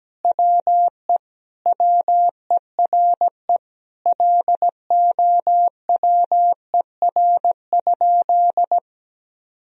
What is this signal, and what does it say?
Morse code, heard as a keyed tone, about 17 wpm, 700 Hz. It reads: WE WERE LOWER?
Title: Sentences from Top 1000 Words - Review Part 4 - 17wpm